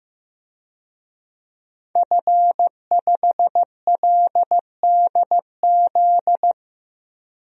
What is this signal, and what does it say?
Morse code, heard as a keyed tone, about 15 wpm, 700 Hz. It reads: F5LDZ
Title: DXCC-100 Call Signs - 5 Characters - 15wpm